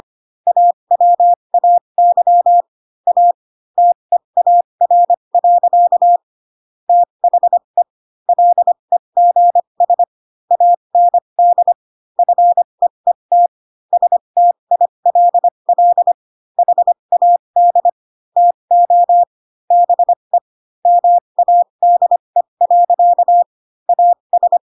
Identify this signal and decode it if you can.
Morse code, heard as a keyed tone, about 25 wpm, spaced as for 22 wpm, 700 Hz. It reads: AWAY A TEAR. THE LEGS AND FEET STILL HAD TO BE MADE. AS
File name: Pinocchio / Pinocchio ch3 18 25 ik1hge